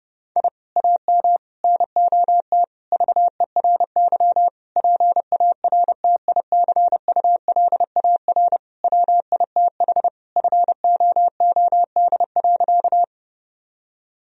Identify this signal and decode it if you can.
Morse code, heard as a keyed tone, about 32 wpm, 700 Hz. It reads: I AM NOT VERY PARTICULAR WITH FOOD.
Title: Sentences from Top 1000 Words - Review Part 4 - 30wpm